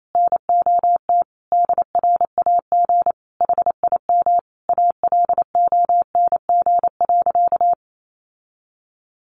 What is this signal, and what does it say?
Morse code, heard as a keyed tone, about 28 wpm, 700 Hz. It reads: NOT DRAG HIM ALONG.
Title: Sentences from Top 1500 Words 28wpm